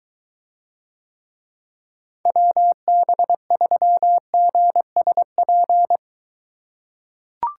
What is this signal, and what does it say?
Morse code, heard as a keyed tone, about 23 wpm, 700 Hz. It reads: WB3GSP